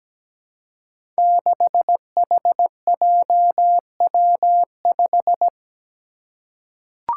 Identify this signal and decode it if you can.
Morse code, heard as a keyed tone, about 17 wpm, 700 Hz. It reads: 6HJW5
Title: Five Challenging Letter-Number Combinations 17wpm